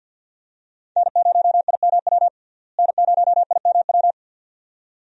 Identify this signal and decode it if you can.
Morse code, heard as a keyed tone, about 52 wpm, 700 Hz. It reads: N0IMW N0IMW
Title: USA Call Signs - 5 Characters - Repeated 2x - 50wpm